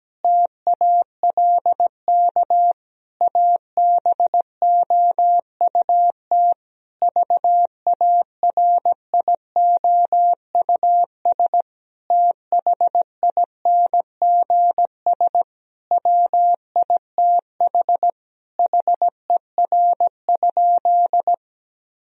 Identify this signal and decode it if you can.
Morse code, heard as a keyed tone, about 17 wpm, 700 Hz. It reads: TALK ABOUT VARIOUS THINGS WITH HER?
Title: Sentences from Top 1000 Words - Review Part 4 - 17wpm